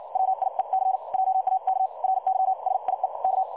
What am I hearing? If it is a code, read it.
Morse code, heard as a keyed tone, about 56 wpm, spaced as for 45 wpm, 760 Hz. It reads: WIEC JAK RYSIEK
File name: dlazaawansowanych / nagraniazpasma / websdr recording 3565.0kHz